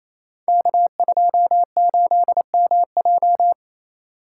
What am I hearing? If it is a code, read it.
Morse code, heard as a keyed tone, about 27 wpm, 690 Hz. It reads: K28MJ
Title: Code Groups - 5 Letter:Numbers - 28wpm